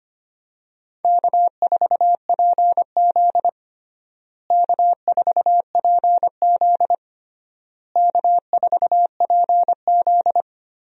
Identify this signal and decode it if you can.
Morse code, heard as a keyed tone, about 25 wpm, 700 Hz. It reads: K4PZ K4PZ K4PZ
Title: USA Call Signs - 4 Characters - Repeated 3x - 25wpm